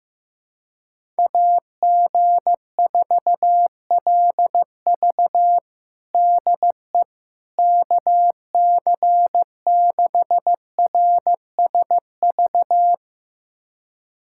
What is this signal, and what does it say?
Morse code, heard as a keyed tone, about 15 wpm, 700 Hz. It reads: AG4LV DE KC6RSV
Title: USA Callsign DE Callsign 15wpm